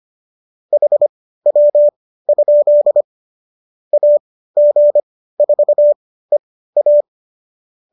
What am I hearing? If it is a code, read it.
Morse code, heard as a keyed tone, about 25 wpm, spaced as for 15 wpm, 590 Hz. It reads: HW? AG4EA